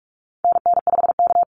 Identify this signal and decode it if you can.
Morse code, heard as a keyed tone, about 44 wpm, 700 Hz. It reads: NN5K
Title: NA SOTA Activator Callsigns 45wpm